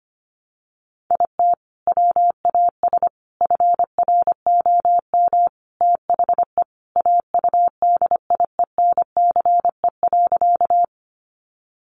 Morse code, heard as a keyed tone, about 25 wpm, 700 Hz. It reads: IT WAS FROM THE AUDIENCE.
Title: Sentences from Top 800 Words 25wpm